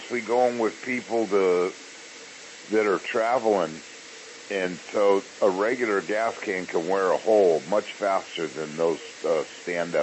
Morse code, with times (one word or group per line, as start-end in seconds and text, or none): none